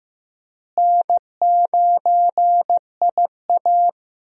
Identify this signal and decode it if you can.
Morse code, heard as a keyed tone, about 15 wpm, 700 Hz. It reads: N9IA
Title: USA Call Signs - 4 Characters - Repeated 2x - 15wpm